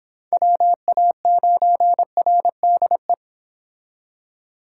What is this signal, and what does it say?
Morse code, heard as a keyed tone, about 30 wpm, 700 Hz. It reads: WA9RDE